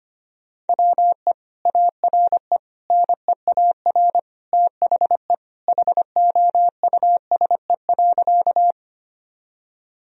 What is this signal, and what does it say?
Morse code, heard as a keyed tone, about 25 wpm, 700 Hz. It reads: WE ARE NEAR THE HOUSE.
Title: Sentences from Top 1000 Words - Review Part 1 - 25wpm